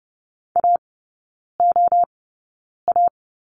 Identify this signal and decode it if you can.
Morse code, heard as a keyed tone, about 30 wpm, spaced as for 9 wpm, 700 Hz. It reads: AOA
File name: Lesson 18e - Letter m - Three Characters 30wpm